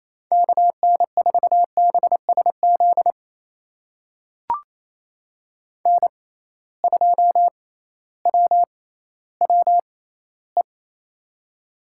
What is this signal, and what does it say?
Morse code, heard as a keyed tone, about 28 wpm, spaced as for 15 wpm, 700 Hz. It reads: KN4BSZ N2WWE
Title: USA Call Signs - 6x Character Spacing - 28wpm